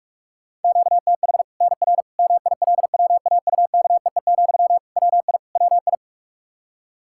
Morse code, heard as a keyed tone, about 44 wpm, 700 Hz. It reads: QTH NR MILWAUKEE, WI WI
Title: K1USN SST QSO Practice 45wpm